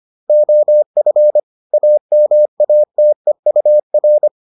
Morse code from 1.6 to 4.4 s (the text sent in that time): AMATEUR